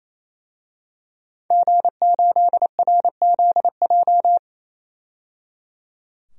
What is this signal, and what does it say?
Morse code, heard as a keyed tone, about 27 wpm, 700 Hz. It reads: G8RZJ
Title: DXCC-100 Call Signs - 5 Characters - 28wpm